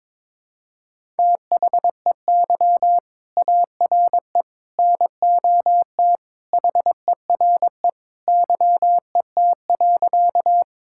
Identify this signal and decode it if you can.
Morse code, heard as a keyed tone, about 22 wpm, 700 Hz. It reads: THEY ARE NOT HERE YET.
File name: practice / 22wpm / Sentences with 2-Word Phrases - 22wpm